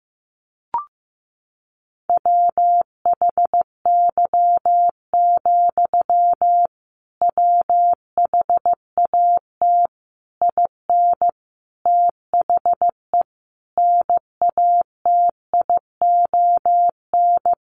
Morse code, heard as a keyed tone, about 15 wpm, 700 Hz. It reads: WHY, WHAT IN THE NATION
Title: Sentences from Top 900 Words 15wpm